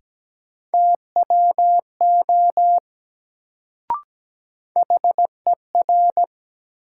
Morse code, heard as a keyed tone, about 17 wpm, 700 Hz. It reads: TWO HER
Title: Sentences from Top 100 Words - New Word Drill - 17wpm